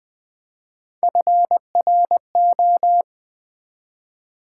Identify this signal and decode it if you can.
Morse code, heard as a keyed tone, about 21 wpm, 700 Hz. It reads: FRO
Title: Build A Word - 1 Letter At A Time - 3 to 5-Letter Words - 20wpm